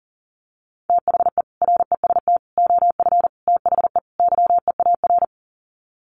Morse code, heard as a keyed tone, about 40 wpm, 700 Hz. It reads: THE REST OF THE YEAR
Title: Sets of 5 Words Encore 40wpm